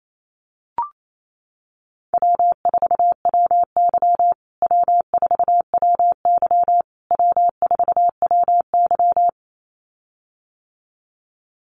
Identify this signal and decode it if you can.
Morse code, heard as a keyed tone, about 28 wpm, 700 Hz. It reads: W4WY W4WY W4WY
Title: USA Call Signs Repeated 3x 28wpm